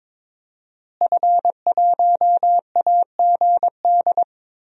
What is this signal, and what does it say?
Morse code, heard as a keyed tone, about 22 wpm, 700 Hz. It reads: F1AGD
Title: DXCC-100 Call Signs - 5 Characters - Repeated 3x - 22wpm